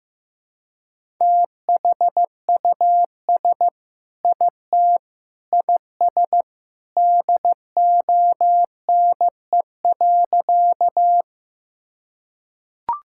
Morse code, heard as a keyed tone, about 15 wpm, 700 Hz. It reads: THUS IT IS DONE.
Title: Sentences from Top 1000 Words - Review Part 5 - 15wpm